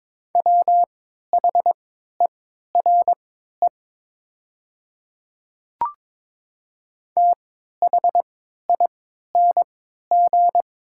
Morse code, heard as a keyed tone, about 22 wpm, spaced as for 13 wpm, 700 Hz. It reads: WHERE THING
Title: 3x Character Spacing - Top 200 Words - 22wpm